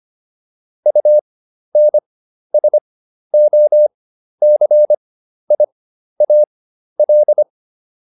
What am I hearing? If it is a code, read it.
Morse code, heard as a keyed tone, about 25 wpm, spaced as for 12 wpm, 600 Hz. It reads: UNSOCIAL